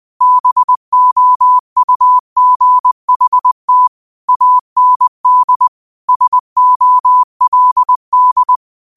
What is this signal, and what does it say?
Morse code, heard as a keyed tone, about 20 wpm, 1000 Hz. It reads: BOUGHT AND SOLD